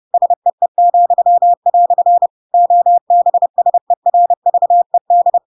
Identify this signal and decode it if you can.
Morse code, heard as a keyed tone, about 30 wpm, 700 Hz. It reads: SEE," OBSERVED
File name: Pinocchio / Pinocchio ch7 30 30 ik1hge